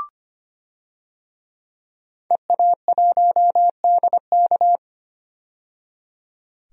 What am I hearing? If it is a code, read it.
Morse code, heard as a keyed tone, about 25 wpm, 700 Hz. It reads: EA1DK
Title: DXCC-100 Call Signs - 5 Characters - 25wpm